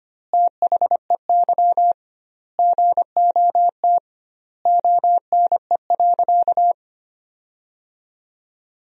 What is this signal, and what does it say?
Morse code, heard as a keyed tone, about 25 wpm, 700 Hz. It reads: THEY GOT ONE.
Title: 2x Word-Spacing - Sentences from Top 300 Words - 25wpm